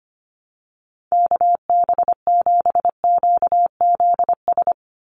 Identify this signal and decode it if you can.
Morse code, heard as a keyed tone, about 25 wpm, 700 Hz. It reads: KB7QZS
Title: USA Call Signs 25wpm